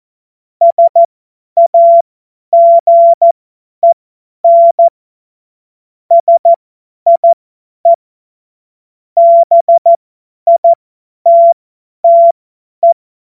Morse code, heard as a keyed tone, about 14 wpm, spaced as for 10 wpm, 680 Hz. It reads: SAGEN SIE BITTE
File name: Witz-01-Juli-2025-50-F70